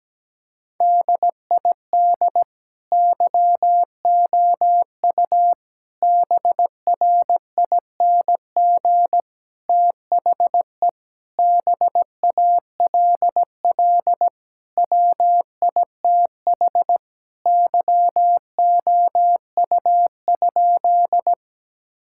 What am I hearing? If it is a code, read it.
Morse code, heard as a keyed tone, about 17 wpm, 700 Hz. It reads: DID YOU BRING THE BALL WITH YOU?
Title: Sentences from Top 1000 Words - Review Part 6 - 17wpm